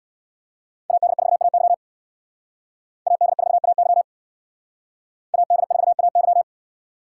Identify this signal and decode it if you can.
Morse code, heard as a keyed tone, about 62 wpm, 700 Hz. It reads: AD4AY AD4AY AD4AY
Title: Speed Racing - USA 2x2 Call Signs 50wpm